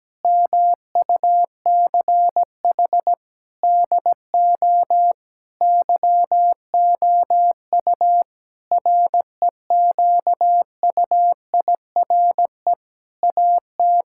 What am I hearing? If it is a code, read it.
Morse code, heard as a keyed tone, about 17 wpm, 700 Hz. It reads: MUCH DO YOU REQUIRE AT
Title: Sentences from Top 1000 Words - Review Part 6 - 17wpm